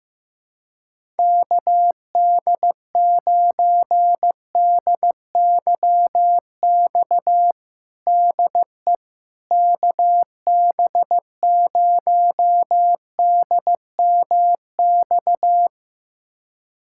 Morse code, heard as a keyed tone, about 15 wpm, 700 Hz. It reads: KD9DYX DE KB0DMX
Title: USA Callsign DE Callsign 15wpm